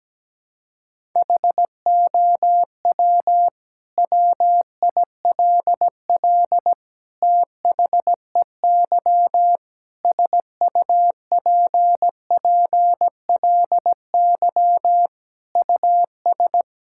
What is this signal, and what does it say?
Morse code, heard as a keyed tone, about 17 wpm, 700 Hz. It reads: HOW WILL THEY SUPPLY US
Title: Sentences from Top 1000 Words - Review Part 5 - 17wpm